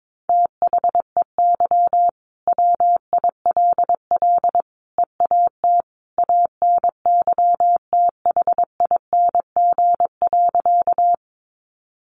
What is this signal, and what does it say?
Morse code, heard as a keyed tone, about 22 wpm, 700 Hz. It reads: THEY WILL EAT ANYTHING.
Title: Sentences from Top 1000 Words 22wpm